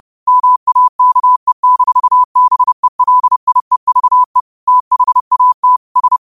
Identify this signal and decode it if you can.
Morse code, heard as a keyed tone, about 30 wpm, 1000 Hz. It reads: MAKE-BELIEVE THAT S
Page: Quotes for Fri, 15 Aug 2025 in Morse Code at 30 words per minute.